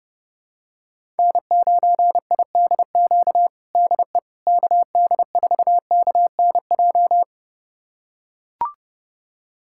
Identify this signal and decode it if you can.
Morse code, heard as a keyed tone, about 30 wpm, 700 Hz. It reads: N9IDQ DE KD4KNJ
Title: USA Callsign DE Callsign 30wpm